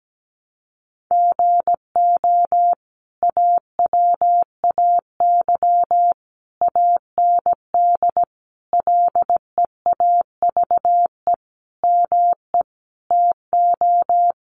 Morse code, heard as a keyed tone, about 17 wpm, 700 Hz. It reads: GO AWAY AND LEAVE ME TO
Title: Sentences from Top 1300 Words 17wpm